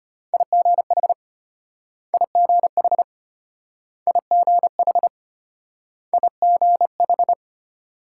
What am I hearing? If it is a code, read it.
Morse code, heard as a keyed tone, about 34 wpm, 700 Hz. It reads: IGH IGH IGH IGH